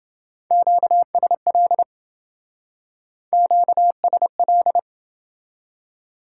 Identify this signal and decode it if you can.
Morse code, heard as a keyed tone, about 28 wpm, 700 Hz. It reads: QSL QSL